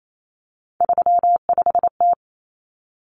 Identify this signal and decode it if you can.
Morse code, heard as a keyed tone, about 28 wpm, 700 Hz. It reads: 35T
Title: Common Cut Numbers - 3-Digit 28wpm